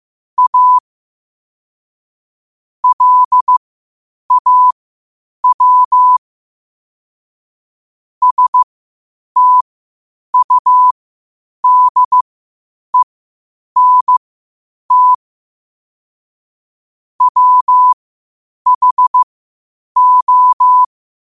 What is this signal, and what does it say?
Morse code, heard as a keyed tone, about 15 wpm, spaced as for 8 wpm, 1000 Hz. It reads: A LAW STUDENT WHO
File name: qotdfiles / 8wpm